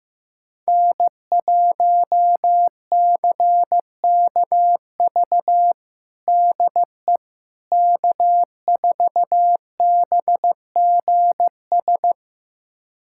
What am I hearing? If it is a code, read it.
Morse code, heard as a keyed tone, about 15 wpm, 700 Hz. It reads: N1CKV DE K4BGS
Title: USA Callsign DE Callsign - 5 Characters - 15wpm